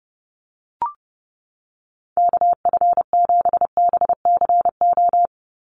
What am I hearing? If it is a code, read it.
Morse code, heard as a keyed tone, about 30 wpm, 700 Hz. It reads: KF7BCO
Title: USA Call Signs 30wpm